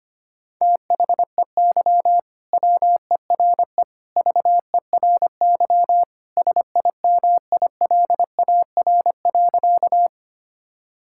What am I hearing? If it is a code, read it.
Morse code, heard as a keyed tone, about 25 wpm, 700 Hz. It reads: THEY WERE VERY SIMILAR.
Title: Sentences from Top 1000 Words - Review Part 5 - 25wpm